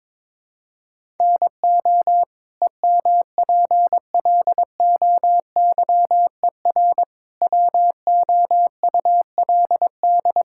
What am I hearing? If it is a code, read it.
Morse code, heard as a keyed tone, about 21 wpm, 700 Hz. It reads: NO EMPLOYER WOULD
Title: Sentences from Top 1600 Words 22wpm